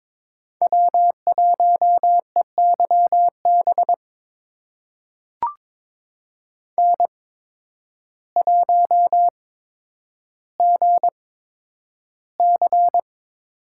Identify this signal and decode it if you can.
Morse code, heard as a keyed tone, about 22 wpm, spaced as for 10 wpm, 700 Hz. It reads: W1EYB N1GC
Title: USA Call Signs - 5 Characters - 8x Character Spacing - 22wpm